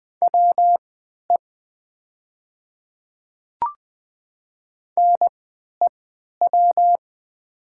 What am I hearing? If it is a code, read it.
Morse code, heard as a keyed tone, about 20 wpm, spaced as for 11 wpm, 700 Hz. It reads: WE NEW
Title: 3x Character Spacing - Top 200 Words - 20wpm